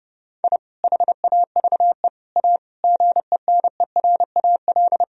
Morse code, heard as a keyed tone, about 32 wpm, 700 Hz. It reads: I HAVE A GENERAL
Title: Sentences from Top 1000 Words - Review Part 4 - 30wpm